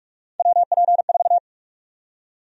Morse code, heard as a keyed tone, about 45 wpm, 700 Hz. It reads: WP4
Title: USA Call Sign Prefixes - 45wpm